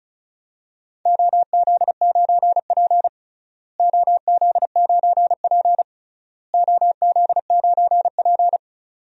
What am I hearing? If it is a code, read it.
Morse code, heard as a keyed tone, about 34 wpm, 700 Hz. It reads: OZ9P OZ9P OZ9P
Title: DXCC-100 Call Signs - 4 Characters - Repeated 3x - 35wpm